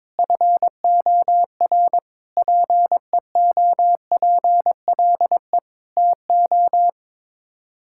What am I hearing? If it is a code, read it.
Morse code, heard as a keyed tone, about 22 wpm, 700 Hz. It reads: FOR PEOPLE TO
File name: practice / 22wpm / Sets of 7 Words - 22wpm